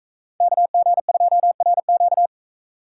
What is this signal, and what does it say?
Morse code, heard as a keyed tone, about 44 wpm, 700 Hz. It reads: KG2RQ